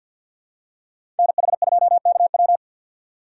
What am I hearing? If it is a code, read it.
Morse code, heard as a keyed tone, about 47 wpm, 700 Hz. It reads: NH2KW
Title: USA Call Signs - 5 Characters - 3x Character Spacing - 50wpm